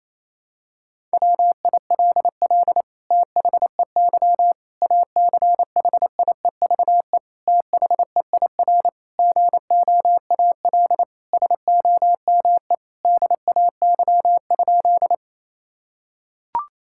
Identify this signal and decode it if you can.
Morse code, heard as a keyed tone, about 28 wpm, 700 Hz. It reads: WILL THEY ACHIEVE THEIR GOAL SOME DAY?
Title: Sentences from Top 1000 Words - Review Part 6 - 28wpm